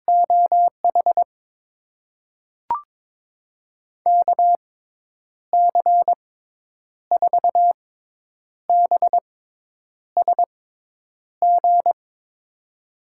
Morse code, heard as a keyed tone, about 22 wpm, spaced as for 9 wpm, 700 Hz. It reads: OH KC4BSG
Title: USA Call Signs - 6 Characters - 6x Character Spacing - 22wpm